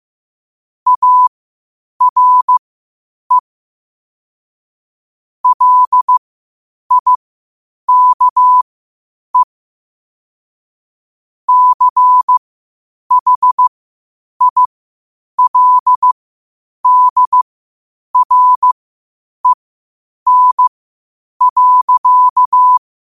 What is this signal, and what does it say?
Morse code, heard as a keyed tone, about 15 wpm, spaced as for 8 wpm, 1000 Hz. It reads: ARE LIKE CHILDREN.
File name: qotdfiles / 8wpm